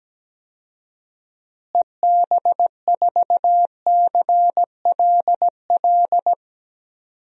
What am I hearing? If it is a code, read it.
Morse code, heard as a keyed tone, about 17 wpm, 700 Hz. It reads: EB4CLL
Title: DXCC-100 Call Signs - 6 Characters - Repeated 3x - 17wpm